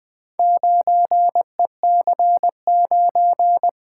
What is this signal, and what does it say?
Morse code, heard as a keyed tone, about 20 wpm, 700 Hz. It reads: 9EC9
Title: Four Letter-Number Combinations - 20wpm